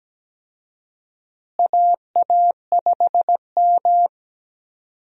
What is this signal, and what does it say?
Morse code, heard as a keyed tone, about 17 wpm, 700 Hz. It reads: AA5M
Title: USA Call Signs - 4 Characters - Repeated 2x - 17wpm